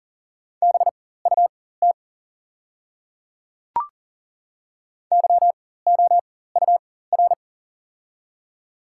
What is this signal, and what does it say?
Morse code, heard as a keyed tone, about 37 wpm, spaced as for 18 wpm, 690 Hz. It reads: BUT YOUR
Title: 4x Character Spacing - Top 500 Words - 40wpm